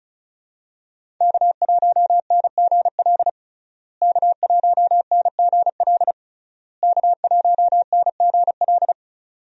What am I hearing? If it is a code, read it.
Morse code, heard as a keyed tone, about 36 wpm, 700 Hz. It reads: K1NGL K1NGL K1NGL
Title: USA Call Signs - 5 Characters - Repeated 3x - 35wpm